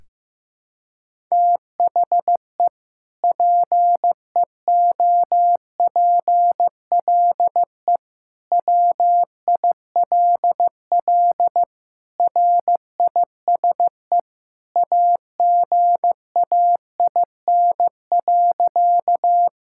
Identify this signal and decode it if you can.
Morse code, heard as a keyed tone, about 15 wpm, 700 Hz. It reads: THE PEOPLE WILL RISE AGAIN.
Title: Sentences from Top 1000 Words - Review Part 6 - 15wpm